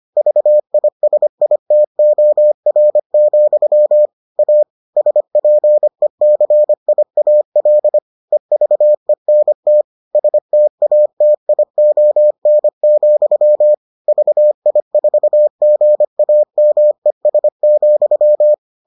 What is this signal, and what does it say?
Morse code, heard as a keyed tone, about 25 wpm, 600 Hz. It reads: VISITOR, A SPECIAL EVENT STATION, VI4GAMES,